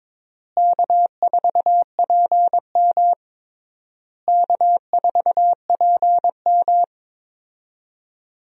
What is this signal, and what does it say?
Morse code, heard as a keyed tone, about 22 wpm, 700 Hz. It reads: K4PM K4PM
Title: USA Call Signs - 4 Characters - Repeated 3x - 22wpm